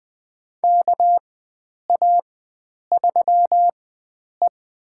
Morse code, heard as a keyed tone, about 21 wpm, spaced as for 9 wpm, 700 Hz. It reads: KA3E
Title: USA Call Signs - 4 Characters - 4x Character Spacing - 20wpm